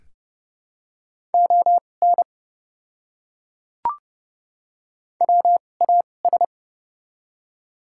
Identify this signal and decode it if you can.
Morse code, heard as a keyed tone, about 29 wpm, spaced as for 22 wpm, 690 Hz. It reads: ON WAS
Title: 2x Character Spacing - Top 100 Words - 30wpm